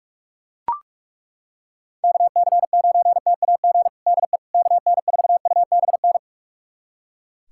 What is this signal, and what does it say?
Morse code, heard as a keyed tone, about 43 wpm, 700 Hz. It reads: KC9TAG DE KN4UBN
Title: USA Callsign DE Callsign - 6 Characters - 45wpm